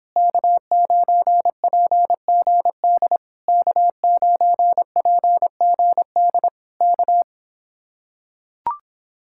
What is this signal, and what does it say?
Morse code, heard as a keyed tone, about 26 wpm, 700 Hz. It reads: K9PGD K9PGD K